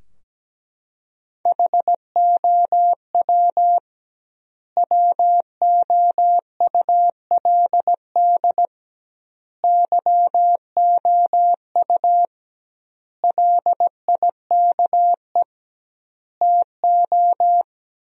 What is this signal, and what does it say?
Morse code, heard as a keyed tone, about 17 wpm, 700 Hz. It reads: HOW WOULD YOU LIKE TO
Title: 2x Word-Spacing - Sentences from Top 100 Words - 17wpm